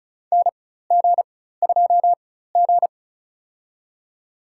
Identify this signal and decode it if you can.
Morse code, heard as a keyed tone, about 36 wpm, spaced as for 16 wpm, 690 Hz. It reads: NG2G
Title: USA Call Signs - 4x Character Spacing - 35wpm